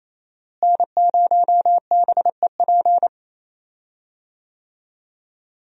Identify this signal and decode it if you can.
Morse code, heard as a keyed tone, about 28 wpm, 700 Hz. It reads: N0BEP
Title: USA Call Signs - 5 Characters - Repeated 3x - 28wpm